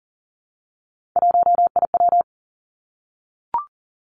Morse code, heard as a keyed tone, about 41 wpm, 690 Hz. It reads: 1IW
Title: Three Challenging Letter-Number Combinations 40wpm